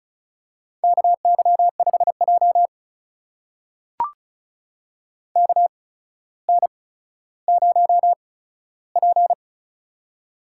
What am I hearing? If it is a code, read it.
Morse code, heard as a keyed tone, about 36 wpm, spaced as for 15 wpm, 700 Hz. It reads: KY5J KN0P
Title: USA Call Signs - 4 Characters - 8x Character Spacing - 35wpm